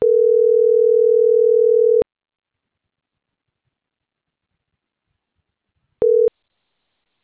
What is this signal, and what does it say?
Ringback tone (North America), 440 + 480 Hz.